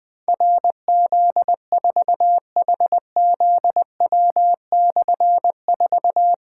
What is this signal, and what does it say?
Morse code, heard as a keyed tone, about 20 wpm, 700 Hz. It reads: RZ4HZW/4
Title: Difficult Call Signs 20wpm